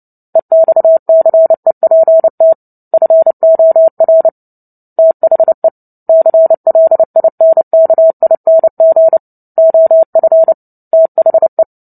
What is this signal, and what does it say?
Morse code, heard as a keyed tone, about 30 wpm, 650 Hz. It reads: EXCEPT FOR THE CLINKING OF THE
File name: Book Two / 03-Chapter 3 - 29 WPM